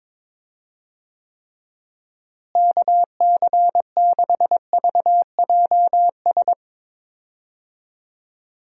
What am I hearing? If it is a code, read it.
Morse code, heard as a keyed tone, about 22 wpm, 700 Hz. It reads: KC6VJS
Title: USA Call Signs - 6 Characters - Repeated 2x - 22wpm